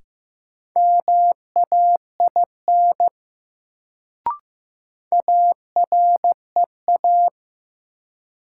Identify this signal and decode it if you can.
Morse code, heard as a keyed tone, about 15 wpm, 700 Hz. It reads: MAIN AREA
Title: Sentences from Top 1000 Words - Review Part 2 - New Word Drill - 15wpm